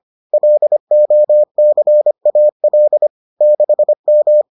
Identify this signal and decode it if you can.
Morse code, heard as a keyed tone, about 25 wpm, 600 Hz. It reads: LOCAL 6M